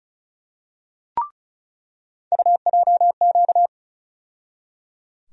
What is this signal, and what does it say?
Morse code, heard as a keyed tone, about 36 wpm, 700 Hz. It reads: UJQ
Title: USA Call Sign Postfixes - 3 Letters 35wpm